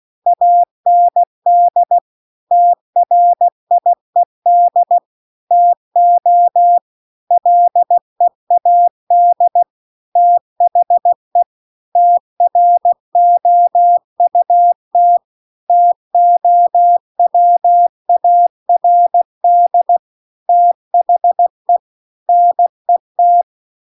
Morse code, heard as a keyed tone, about 16 wpm, 700 Hz. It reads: AND TRIED TO LEAD THE TROUT TOWARD THE NET